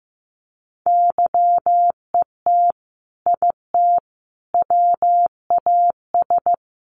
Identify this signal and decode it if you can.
Morse code, heard as a keyed tone, about 15 wpm, 700 Hz. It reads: YET IT WAS
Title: Sentences from Top 800 Words 15wpm